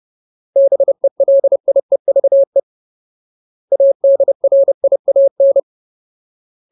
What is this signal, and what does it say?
Morse code, heard as a keyed tone, about 30 wpm, 550 Hz. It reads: BELIEVE ADRIAN
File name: Ultimate Word List / 30WPM Ultimate